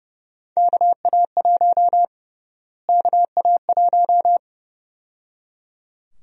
Morse code, heard as a keyed tone, about 32 wpm, 700 Hz. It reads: KA1 KA1
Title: USA Call Sign Prefixes - Repeated 2x - 30wpm